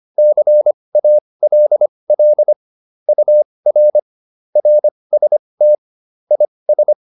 Morse code, heard as a keyed tone, about 25 wpm, spaced as for 20 wpm, 600 Hz. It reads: CALL UR RST IS